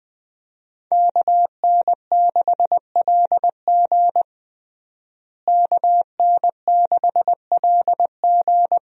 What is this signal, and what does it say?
Morse code, heard as a keyed tone, about 20 wpm, 700 Hz. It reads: KN6LG KN6LG
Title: USA Call Signs - 5 Characters - Repeated 2x - 20wpm